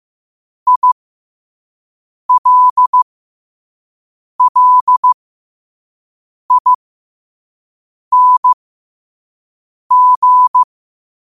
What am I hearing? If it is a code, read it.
Morse code, heard as a keyed tone, about 15 wpm, spaced as for 5 wpm, 1000 Hz. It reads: ILLING